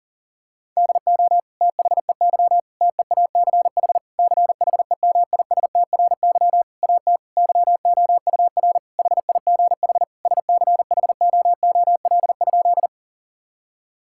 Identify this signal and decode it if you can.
Morse code, heard as a keyed tone, about 40 wpm, 700 Hz. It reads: DO THEY TEACH CHEMISTRY AT YOUR HIGH SCHOOL?
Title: Sentences from Top 1000 Words - Review Part 6 - 40wpm